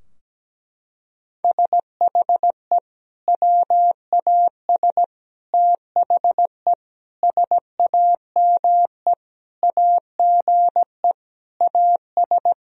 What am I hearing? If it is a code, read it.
Morse code, heard as a keyed tone, about 17 wpm, 700 Hz. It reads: SHE WAS THE SAME AGE AS
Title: Sentences from Top 1000 Words - Review Part 3 - 17wpm